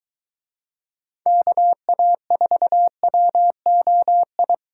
Morse code, heard as a keyed tone, about 22 wpm, 700 Hz. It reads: KA4WOI